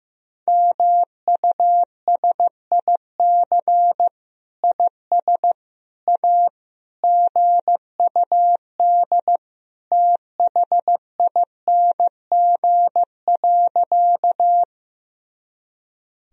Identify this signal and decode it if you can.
Morse code, heard as a keyed tone, about 15 wpm, 700 Hz. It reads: MUSIC IS A GUD THING.
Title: Sentences from Top 300 Words and Abbreviations 15wpm